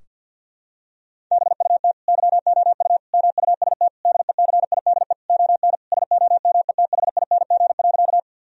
Morse code, heard as a keyed tone, about 47 wpm, 700 Hz. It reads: BUT YOU MUST DECIDE ON SOMETHING.
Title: Sentences from Top 1000 Words - Review Part 3 - 50wpm